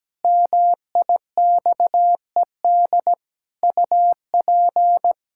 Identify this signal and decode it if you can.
Morse code, heard as a keyed tone, about 17 wpm, 700 Hz. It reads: MIXED UP
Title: Pangram Challenge v2 - 17wpm